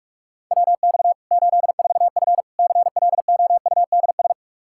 Morse code, heard as a keyed tone, about 45 wpm, 700 Hz. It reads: WX 84F CLOUDS
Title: Practice QSO - Exchange 2 - Info - 45wpm